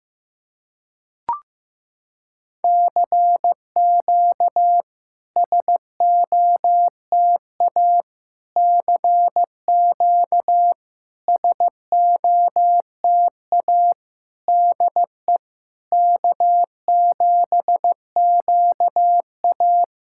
Morse code, heard as a keyed tone, about 15 wpm, 700 Hz. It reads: CQ SOTA CQ SOTA DE K7QA
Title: SOTA QSO Practice 15wpm